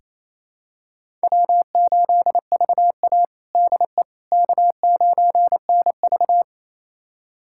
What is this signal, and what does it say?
Morse code, heard as a keyed tone, about 28 wpm, 700 Hz. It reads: W8VA DE K9NV
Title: USA Callsign DE Callsign - 4 Characters - 28wpm